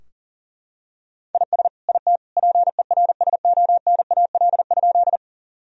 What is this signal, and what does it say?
Morse code, heard as a keyed tone, about 40 wpm, 700 Hz. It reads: IS IT PERSONAL?
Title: Sentences from Top 1000 Words - Review Part 4 - 40wpm